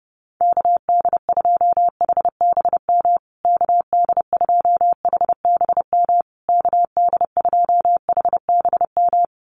Morse code, heard as a keyed tone, about 30 wpm, 700 Hz. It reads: KD2HBM KD2HBM KD2HBM
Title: USA Call Signs Repeated 3x 30wpm